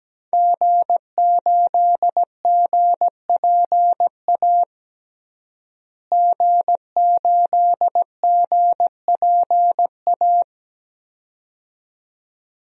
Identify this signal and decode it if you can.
Morse code, heard as a keyed tone, about 17 wpm, 700 Hz. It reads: G8GPA G8GPA
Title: DXCC-100 Call Signs - 5 Characters - Repeated 2x - 17wpm